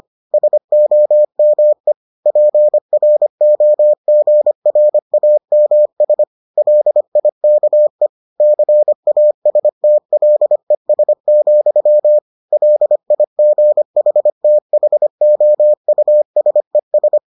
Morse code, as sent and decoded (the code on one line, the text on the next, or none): ... --- -- . / .--. .-. --- --. .-. .- -- ... / .-.. .. -.- . / -.-. .- ... - .-.. ... --..-- / .-.. .. --. .... - .... --- ..- ... ...
SOME PROGRAMS LIKE CASTLES, LIGHTHOUSES